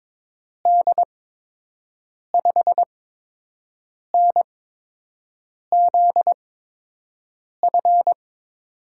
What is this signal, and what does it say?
Morse code, heard as a keyed tone, about 22 wpm, spaced as for 6 wpm, 700 Hz. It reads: D5NZF
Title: USA Call Signs - 6 Characters - 8x Character Spacing - 22wpm